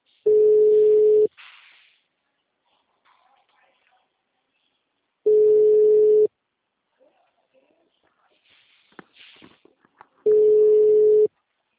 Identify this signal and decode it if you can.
Ringback tone (Europe, 425 Hz).
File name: out- -2006- - - .434